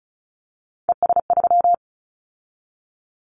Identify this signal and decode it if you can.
Morse code, heard as a keyed tone, about 36 wpm, 700 Hz. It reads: ES3
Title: Callsign Prefixes for Top 50 DXCC - 35wpm